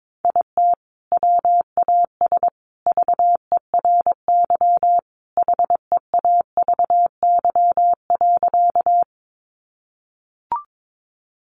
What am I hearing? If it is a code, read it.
Morse code, heard as a keyed tone, about 22 wpm, 700 Hz. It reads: IT WAS VERY HEAVY.
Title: Sentences from Top 1100 Words 22wpm